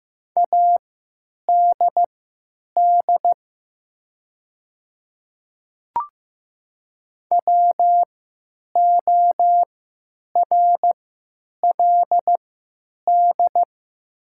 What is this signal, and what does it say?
Morse code, heard as a keyed tone, about 15 wpm, spaced as for 9 wpm, 700 Hz. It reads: ADD WORLD
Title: 3x Character Spacing - Top 200 Words - 15wpm